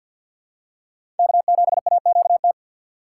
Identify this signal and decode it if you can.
Morse code, heard as a keyed tone, about 52 wpm, 700 Hz. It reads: K7AQT
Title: USA Call Signs - 5 Characters - 6x Character Spacing - 50wpm